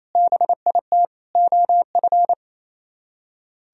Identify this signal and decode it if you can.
Morse code, heard as a keyed tone, about 34 wpm, 700 Hz. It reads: BIT OF
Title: Speed-Racing - Sets of 4 Words 28wpm